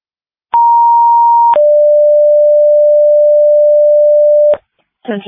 Two-tone page, heard as a one-second tone, then a three-second tone, about 940 then 590 Hz. It is heard from 0.5 to 4.5 s.
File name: dispatch / Tripoli Ambulance 2026 07 13 10 38 10